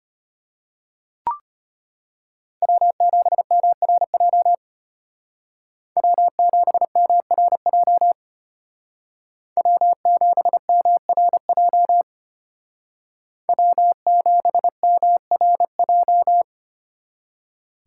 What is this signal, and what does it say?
Morse code, heard as a keyed tone, about 30 wpm, 700 Hz. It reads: W7MRJ W7MRJ W7MRJ W7MRJ
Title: Speed Racing - USA Call Signs 25wpm